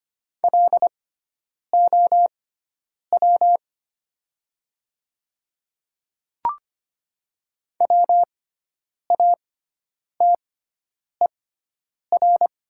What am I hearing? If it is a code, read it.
Morse code, heard as a keyed tone, about 25 wpm, spaced as for 9 wpm, 700 Hz. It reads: LOW WATER
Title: 6x Character Spacing - Top 200 Words - 25wpm